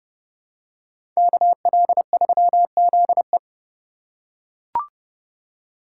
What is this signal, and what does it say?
Morse code, heard as a keyed tone, about 30 wpm, 700 Hz. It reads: KL3ZE